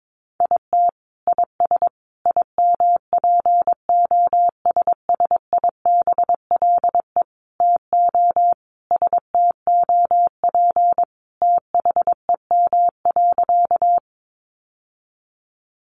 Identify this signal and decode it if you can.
Morse code, heard as a keyed tone, about 22 wpm, 700 Hz. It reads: IT IS IMPOSSIBLE TO STOP THEM.
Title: Sentences from Top 1100 Words 22wpm